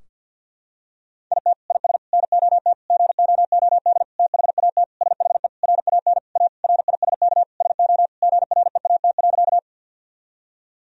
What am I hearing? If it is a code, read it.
Morse code, heard as a keyed tone, about 50 wpm, 700 Hz. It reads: IT IS NOT GOOD THAT SHE RAN A RISK SO GREAT.
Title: Sentences from Top 1000 Words - Review Part 4 - 50wpm